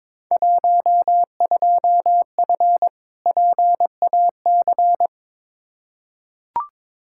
Morse code, heard as a keyed tone, about 22 wpm, 700 Hz. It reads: 12F PAC
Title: Field Day Exchanges Encore - 22wpm